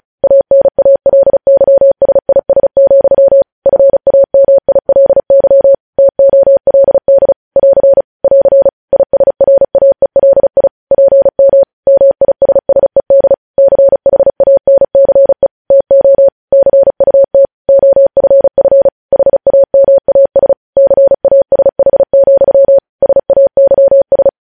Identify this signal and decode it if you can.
Morse code, heard as a keyed tone, about 36 wpm, 550 Hz. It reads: ANALYSIS, FAMILY TOLD <AR> <AR> ISRAELI PM MISSED CHANCE TO CUT OFF HAMAS CASH, SAYS
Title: News Headlines at 35